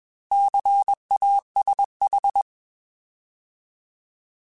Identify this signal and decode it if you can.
Morse code, heard as a keyed tone, about 21 wpm, 800 Hz. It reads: CASH